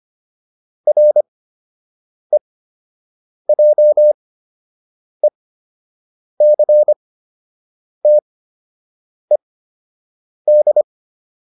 Morse code, heard as a keyed tone, about 25 wpm, spaced as for 7 wpm, 600 Hz. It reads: REJECTED
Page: fundamental-practice-files